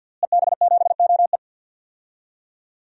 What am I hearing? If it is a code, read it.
Morse code, heard as a keyed tone, about 52 wpm, 700 Hz. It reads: EB7OE
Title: DXCC-100 Call Signs - 5 Characters - Repeated 2x - 50wpm